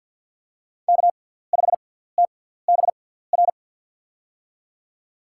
Morse code, heard as a keyed tone, about 54 wpm, spaced as for 17 wpm, 700 Hz. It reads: K5TBR
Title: USA Call Signs - 6x Character Spacing - 50wpm